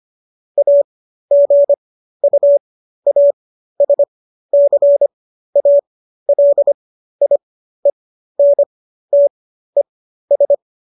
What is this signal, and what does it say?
Morse code, heard as a keyed tone, about 25 wpm, spaced as for 13 wpm, 570 Hz. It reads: AGUASCALIENTES